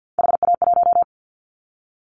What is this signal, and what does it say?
Morse code, heard as a keyed tone, about 50 wpm, 700 Hz. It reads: HA1
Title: Callsign Prefixes for Top 50 DXCC - 50wpm